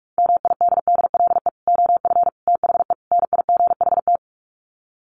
Morse code, heard as a keyed tone, about 44 wpm, 700 Hz. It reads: MIDDLE OF THE NIGHT
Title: Sets of 4 Words 45wpm